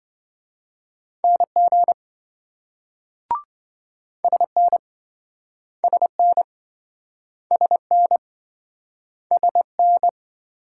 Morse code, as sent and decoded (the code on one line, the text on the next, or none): -. --. / ... -. / ... -. / ... -. / ... -.
NG SN SN SN SN